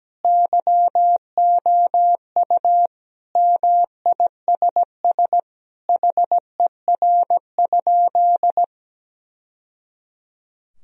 Morse code, heard as a keyed tone, about 17 wpm, 700 Hz. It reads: YOU MISS HER?
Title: Sentences from Top 1000 Words - Review Part 3 - 17wpm